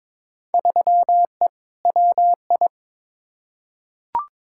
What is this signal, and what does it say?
Morse code, heard as a keyed tone, about 22 wpm, 700 Hz. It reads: 3E WI
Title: Field Day Exchanges Encore - 22wpm